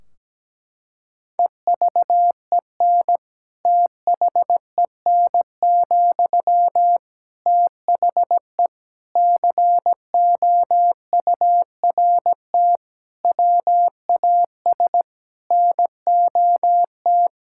Morse code, heard as a keyed tone, about 17 wpm, 700 Hz. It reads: EVEN THEN, THE COURT WAS NOT
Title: Sentences from Top 1600 Words 17wpm